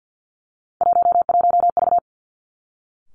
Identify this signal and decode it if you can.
Morse code, heard as a keyed tone, about 50 wpm, 700 Hz. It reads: 11V
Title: Three Challenging Letter-Number Combinations 50wpm